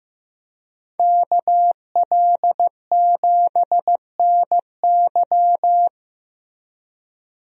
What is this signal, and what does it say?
Morse code, heard as a keyed tone, about 15 wpm, 700 Hz. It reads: KL7NY
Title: USA Call Signs - 5 Characters - Repeated 3x - 15wpm